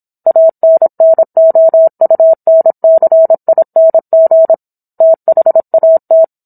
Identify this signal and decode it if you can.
Morse code, heard as a keyed tone, about 26 wpm, 650 Hz. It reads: ANNOUNCING THAT